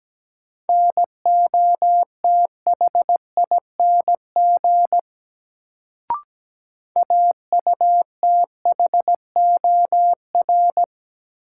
Morse code, heard as a keyed tone, about 17 wpm, 700 Hz. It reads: NOTHING AUTHOR
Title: Sets of 4 Words - New Word Drill - 17wpm